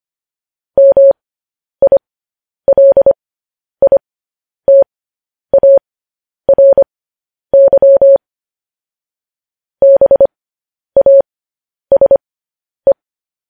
Morse code, heard as a keyed tone, about 25 wpm, spaced as for 10 wpm, 550 Hz. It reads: MILITARY BASE
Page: news headlines in morse code at 10 wpm